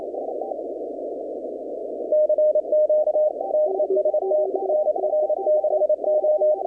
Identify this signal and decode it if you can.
Morse code, heard as a keyed tone, about 29 wpm, 610 Hz, with noise. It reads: CQ TEST NR6O
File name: contests / 2019-cw-wpx-kt5j-k5pi- -2330 0-20